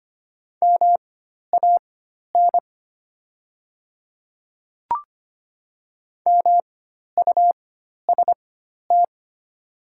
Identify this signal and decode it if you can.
Morse code, heard as a keyed tone, about 25 wpm, spaced as for 12 wpm, 700 Hz. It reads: MAN MUST
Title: 4x Character Spacing - Top 200 Words - 25wpm